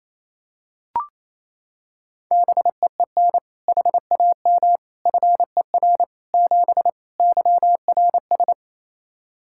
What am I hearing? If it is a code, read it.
Morse code, heard as a keyed tone, about 28 wpm, 700 Hz. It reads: BEEN HAM FER 7 YRS
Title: K1USN SST QSO Practice 28wpm